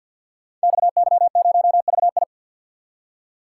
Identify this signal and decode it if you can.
Morse code, heard as a keyed tone, about 51 wpm, 700 Hz. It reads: XY0VI
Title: Code Groups - 5 Letter:Numbers - 50wpm